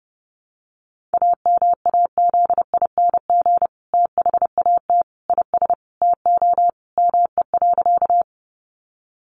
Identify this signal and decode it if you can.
Morse code, heard as a keyed tone, about 30 wpm, 700 Hz. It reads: AMAZING THAT IS TO ME.
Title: Sentences from Top 1500 Words 30wpm